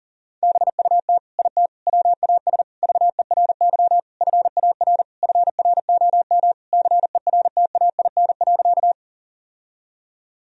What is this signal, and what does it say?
Morse code, heard as a keyed tone, about 40 wpm, 700 Hz. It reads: BUT IT WAS VERY FAR FROM CERTAIN.
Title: Sentences from Top 1000 Words - Review Part 2 - 40wpm